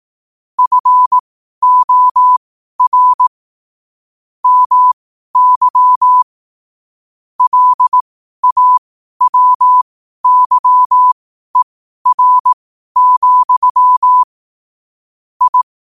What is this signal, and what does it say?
Morse code, heard as a keyed tone, about 18 wpm, spaced as for 12 wpm, 1000 Hz. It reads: FOR MY LAWYER, I
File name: qotdfiles / 12wpm